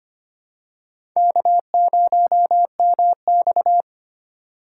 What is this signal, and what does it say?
Morse code, heard as a keyed tone, about 25 wpm, 700 Hz. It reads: K0MX
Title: USA Call Signs - 6x Character Spacing - 25wpm